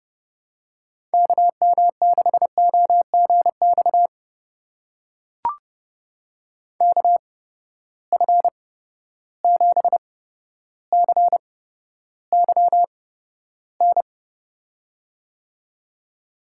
Morse code, heard as a keyed tone, about 32 wpm, spaced as for 13 wpm, 700 Hz. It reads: KM6OGX KF7CYN
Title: USA Call Signs - 6 Characters - 8x Character Spacing - 30wpm